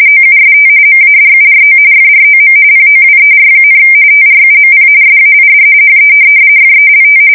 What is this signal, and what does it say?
RTTY, about 45.45 baud. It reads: OUST MADURO FROM POWER, TRUMP SAID HE FELT
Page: 170 Shift 2125/2295